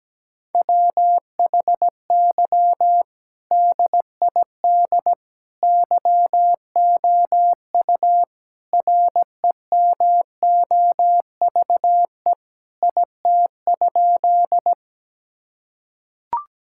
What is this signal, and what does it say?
Morse code, heard as a keyed tone, about 17 wpm, 700 Hz. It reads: WHY DID YOU REMOVE IT?
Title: Sentences from Top 1000 Words - Review Part 5 - 17wpm